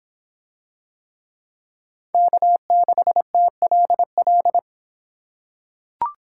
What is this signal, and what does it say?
Morse code, heard as a keyed tone, about 26 wpm, 700 Hz. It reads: K6TLL